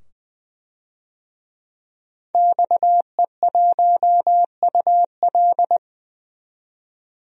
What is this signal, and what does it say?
Morse code, heard as a keyed tone, about 20 wpm, 700 Hz. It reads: XE1UL
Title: DXCC-100 Call Signs - 5 Characters - 20wpm